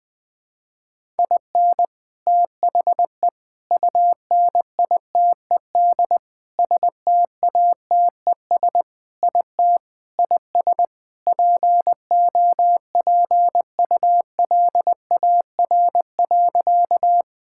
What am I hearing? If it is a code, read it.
Morse code, heard as a keyed tone, about 20 wpm, 700 Hz. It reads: IN THE UNITED STATES IT IS POPULAR.
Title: Sentences with 3-Word Phrases - 20wpm